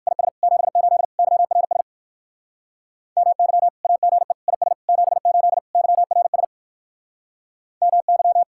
Morse code, heard as a keyed tone, about 55 wpm, 700 Hz. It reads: IS 78 YRS MY AGE IS 78 YRS MY
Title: Speed-Racing Common QSO Phrases - 45wpm